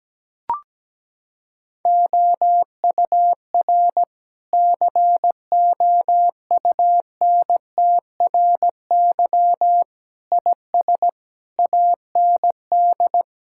Morse code, heard as a keyed tone, about 17 wpm, 700 Hz. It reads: OUR COUNTRY IS AND
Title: Sentences from Top 1000 Words - Review Part 6 - 17wpm